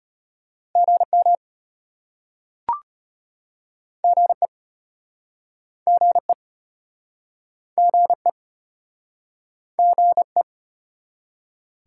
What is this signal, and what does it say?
Morse code, heard as a keyed tone, about 35 wpm, 700 Hz. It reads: GM GE GE GE GE